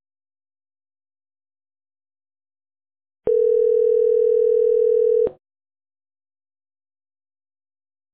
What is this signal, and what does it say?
Ringback tone (North America), 440 + 480 Hz.